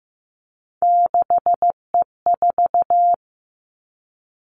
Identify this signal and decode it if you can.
Morse code, heard as a keyed tone, about 15 wpm, 700 Hz. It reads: 6E4
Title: Three Challenging Letter-Number Combinations 15wpm